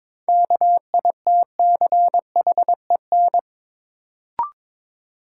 Morse code, heard as a keyed tone, about 22 wpm, 700 Hz. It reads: KITCHEN
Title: Sentences from Top 1400 Words - New Word Drill - 22wpm